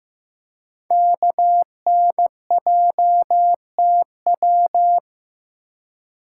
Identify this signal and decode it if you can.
Morse code, heard as a keyed tone, about 15 wpm, 700 Hz. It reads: KNJTW
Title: Code Groups - 5 Letter:Numbers - 15wpm